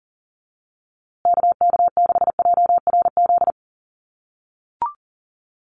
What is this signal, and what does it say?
Morse code, heard as a keyed tone, about 40 wpm, 700 Hz. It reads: KK6JRZ
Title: USA Call Signs Repeated 2x 40wpm